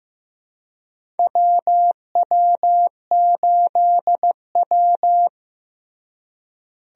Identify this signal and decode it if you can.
Morse code, heard as a keyed tone, about 15 wpm, 700 Hz. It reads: WW8W
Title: USA Call Signs - 4 Characters - Repeated 3x - 15wpm